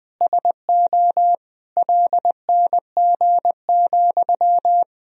Morse code, heard as a keyed tone, about 20 wpm, 700 Hz. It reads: SO LNG,
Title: Sentences from Top 500 Words and Abbreviations 20wpm